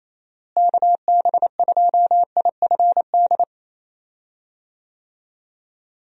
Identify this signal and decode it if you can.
Morse code, heard as a keyed tone, about 28 wpm, 700 Hz. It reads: KB2IFD
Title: USA Call Signs - 6 Characters - Repeated 2x - 28wpm